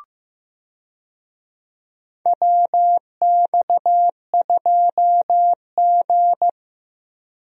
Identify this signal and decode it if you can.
Morse code, heard as a keyed tone, about 15 wpm, 700 Hz. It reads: WX2G
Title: USA Call Signs - 4 Characters - Repeated 2x - 15wpm